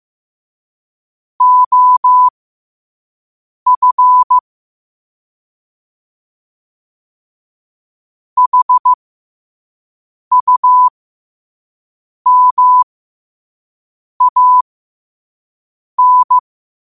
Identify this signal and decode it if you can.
Morse code, heard as a keyed tone, about 15 wpm, spaced as for 5 wpm, 1000 Hz. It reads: OF HUMAN
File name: qotdfiles / 5wpm